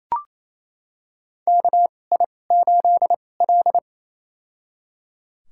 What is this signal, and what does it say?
Morse code, heard as a keyed tone, about 28 wpm, spaced as for 20 wpm, 700 Hz. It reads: KI8L
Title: USA Call Signs - 4 Characters - 2x Character Spacing - 28wpm